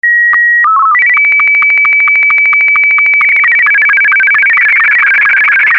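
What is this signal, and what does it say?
SSTV (slow-scan television), Martin 2.
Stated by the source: Analog SSTV signal, mode M2.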